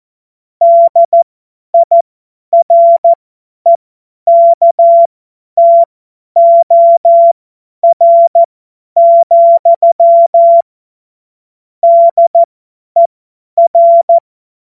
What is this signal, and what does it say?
Morse code, heard as a keyed tone, about 14 wpm, spaced as for 10 wpm, 680 Hz. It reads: DIREKTOR, DER
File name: Morsekurs / Witze / Witz 23. Mai 2023 50 F70